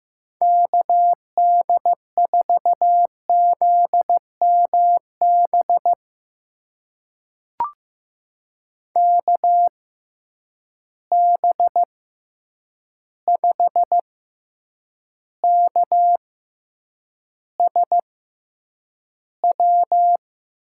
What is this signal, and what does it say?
Morse code, heard as a keyed tone, about 15 wpm, spaced as for 8 wpm, 700 Hz. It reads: KD4ZMB KB5KSW
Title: USA Call Signs - 6 Characters - 6x Character Spacing - 15wpm